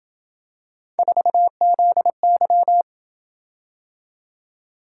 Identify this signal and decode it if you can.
Morse code, heard as a keyed tone, about 26 wpm, 700 Hz. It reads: 4ZY